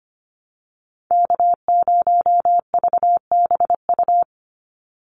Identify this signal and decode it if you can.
Morse code, heard as a keyed tone, about 25 wpm, 700 Hz. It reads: K0VBU
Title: Call Signs - 25wpm